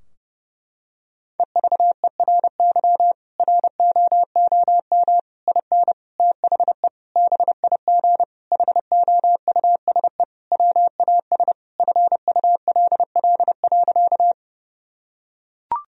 Morse code, heard as a keyed tone, about 30 wpm, 700 Hz. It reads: EVERY ROOM IN THE BIG HOUSE WAS FULL.
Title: Sentences from Top 1000 Words - Review Part 3 - 30wpm